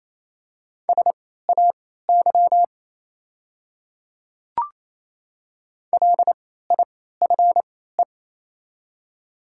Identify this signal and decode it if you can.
Morse code, heard as a keyed tone, about 28 wpm, spaced as for 16 wpm, 700 Hz. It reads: SAY LIFE
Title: 3x Character Spacing - Top 300 Words - 28wpm